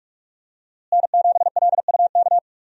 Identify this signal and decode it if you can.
Morse code, heard as a keyed tone, about 46 wpm, 700 Hz. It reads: N7LUK
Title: USA Call Signs - 5 Characters - Repeated 3x - 45wpm